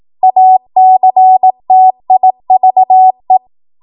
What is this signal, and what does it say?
Morse code, heard as a keyed tone, about 18 wpm, 750 Hz. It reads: ACTIVE
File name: Morse / 18WPM